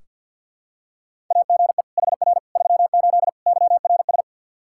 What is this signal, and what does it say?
Morse code, heard as a keyed tone, about 46 wpm, 700 Hz. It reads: AGE HR 38 YRS
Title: Practice QSO - Exchange 3 - More Info - 50wpm